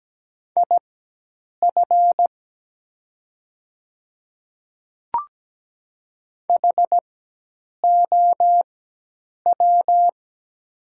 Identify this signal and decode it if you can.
Morse code, heard as a keyed tone, about 17 wpm, spaced as for 8 wpm, 700 Hz. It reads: IF HOW
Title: 4x Character Spacing - Top 100 Words - 17wpm